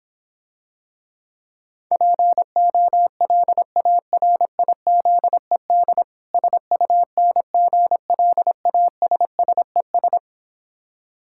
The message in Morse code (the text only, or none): POLARIZED SUNGLASSES